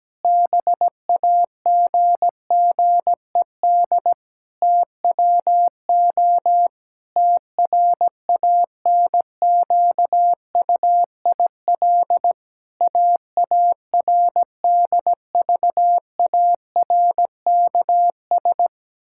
Morse code, heard as a keyed tone, about 17 wpm, 700 Hz. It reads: BAGGED TWO TRANQUIL AARDVARKS
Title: Pangram Challenge v2 - 17wpm